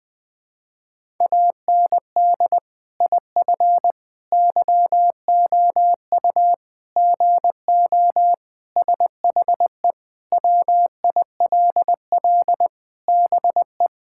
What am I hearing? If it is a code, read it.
Morse code, heard as a keyed tone, about 21 wpm, 700 Hz. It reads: AND IF YOU GO SHE WILL BE
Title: Sentences with 3-Word Phrases - 20wpm